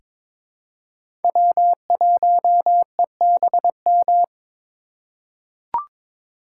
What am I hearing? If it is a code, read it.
Morse code, heard as a keyed tone, about 22 wpm, 700 Hz. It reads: W1EBM
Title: K1USN SST QSO Practice - 22wpm